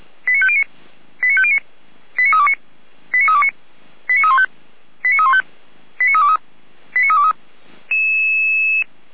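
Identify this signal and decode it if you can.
Five-tone selective call (ZVEI), heard as quick runs of five tones, tones of about 70 ms. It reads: 89509 89509 89329 89329 89315 89315 89323 89323
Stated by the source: ZVEI five-tone selective call 89333 received by SDR.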